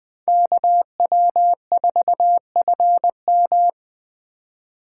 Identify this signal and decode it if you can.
Morse code, heard as a keyed tone, about 20 wpm, 700 Hz. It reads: KW4FM
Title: 2023 POTA Hunter Call Signs - 20wpm